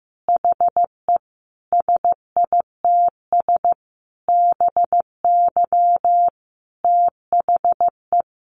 Morse code, heard as a keyed tone, about 15 wpm, 700 Hz. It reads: HE SITS BY THE
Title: Easy Phrases 15wpm